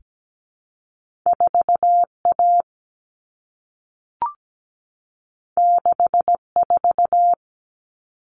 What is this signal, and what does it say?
Morse code, heard as a keyed tone, about 17 wpm, 700 Hz. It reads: 4A 64
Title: Common Cut Numbers - 2-Digit 17wpm